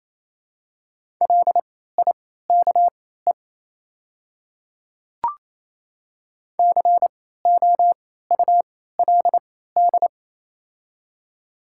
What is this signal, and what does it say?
Morse code, heard as a keyed tone, about 28 wpm, spaced as for 16 wpm, 700 Hz. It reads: LIKE COULD
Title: 3x Character Spacing - Top 200 Words - 28wpm